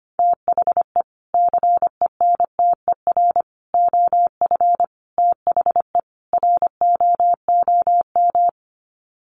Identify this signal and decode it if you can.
Morse code, heard as a keyed tone, about 25 wpm, 700 Hz. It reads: THE CENTER OF THE ROOM
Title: Sets of 5 Words Encore 25wpm